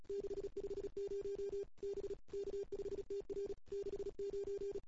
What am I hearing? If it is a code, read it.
Morse code, heard as a keyed tone, about 37 wpm, 380 Hz. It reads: 650 B K5TR 69
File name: CW / 2008-ss-cw-k5tr-op-n5rz -03:00